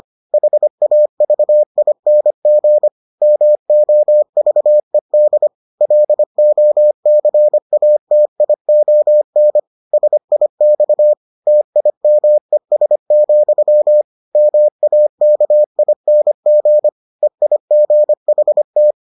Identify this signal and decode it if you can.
Morse code, heard as a keyed tone, about 25 wpm, 600 Hz. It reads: HAVING MOVED LOCATION SIX TIMES, MAKING EIGHT